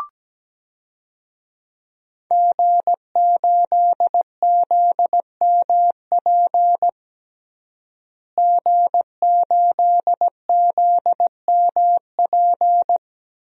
Morse code, heard as a keyed tone, about 17 wpm, 700 Hz. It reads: G8ZMP G8ZMP
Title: DXCC-100 Call Signs - 5 Characters - Repeated 3x - 17wpm